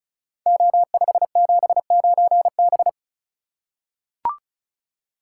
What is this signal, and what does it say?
Morse code, heard as a keyed tone, about 36 wpm, 700 Hz. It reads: O579B
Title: Code Groups - 5 Letter:Numbers - 35wpm